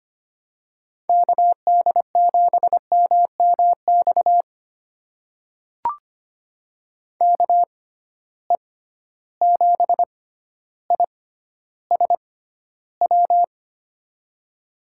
Morse code, heard as a keyed tone, about 25 wpm, spaced as for 13 wpm, 700 Hz. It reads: KD7MMX KE7ISW
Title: USA Call Signs - 6 Characters - 6x Character Spacing - 25wpm